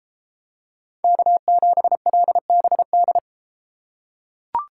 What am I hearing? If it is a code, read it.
Morse code, heard as a keyed tone, about 33 wpm, 700 Hz. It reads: K7LBD